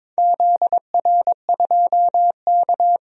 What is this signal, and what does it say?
Morse code, heard as a keyed tone, about 22 wpm, 700 Hz. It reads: ZR2K
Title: DXCC-100 Call Signs - 4 Characters - 22wpm